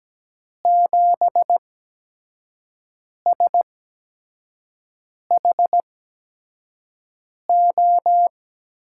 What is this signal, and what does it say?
Morse code, heard as a keyed tone, about 17 wpm, spaced as for 5 wpm, 700 Hz. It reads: 7SHO
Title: USA Call Signs - 6 Characters - 8x Character Spacing - 17wpm